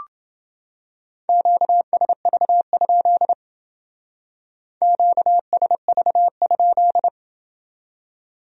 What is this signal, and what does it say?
Morse code, heard as a keyed tone, about 28 wpm, 700 Hz. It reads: QSV? QSV?